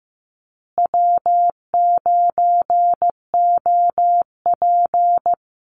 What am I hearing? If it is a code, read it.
Morse code, heard as a keyed tone, about 15 wpm, 700 Hz. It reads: W9OP
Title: Call Signs Repeated 2x - 15wpm